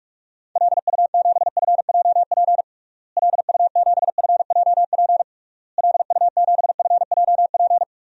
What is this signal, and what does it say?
Morse code, heard as a keyed tone, about 47 wpm, 700 Hz. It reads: LU7FJP LU7FJP LU7FJP
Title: DXCC-100 Call Signs - 6 Characters - Repeated 3x - 45wpm